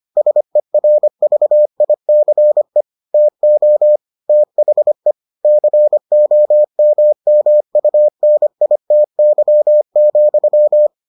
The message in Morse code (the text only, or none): SERVICE TO THE COMMUNITY,